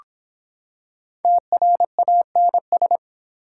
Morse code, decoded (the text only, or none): TRANS